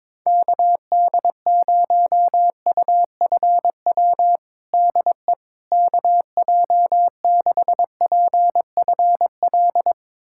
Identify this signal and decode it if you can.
Morse code, heard as a keyed tone, about 21 wpm, 700 Hz. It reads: KD0UFW DE KJ6PFL
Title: USA Callsign DE Callsign - 6 Characters - 22wpm